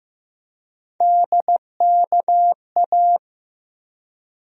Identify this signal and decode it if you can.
Morse code, heard as a keyed tone, about 15 wpm, 700 Hz. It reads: DKA
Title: USA Call Sign Postfixes - 3 Letters 15wpm